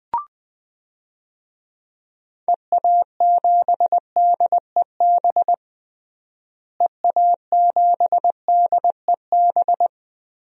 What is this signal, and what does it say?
Morse code, heard as a keyed tone, about 20 wpm, 700 Hz. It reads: EA7DEB EA7DEB
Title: DXCC-100 Call Signs - 6 Characters - Repeated 2x - 20wpm